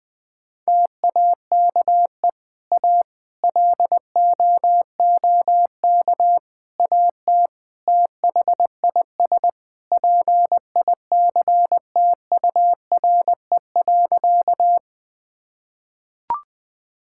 Morse code, heard as a keyed tone, about 20 wpm, 700 Hz. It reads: TAKE A LOOK AT THIS PICTURE.
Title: Sentences with 4-Word Phrases - 20wpm